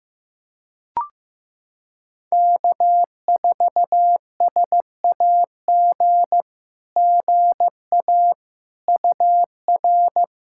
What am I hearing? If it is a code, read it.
Morse code, heard as a keyed tone, about 15 wpm, 700 Hz. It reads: K4SAG GA UR
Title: SOTA QSO Practice 15wpm